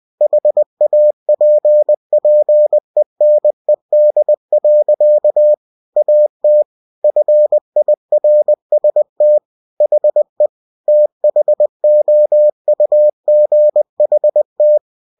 Morse code, heard as a keyed tone, about 20 wpm, 600 Hz. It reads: HAPPENED. AT FIRST HE THOUGHT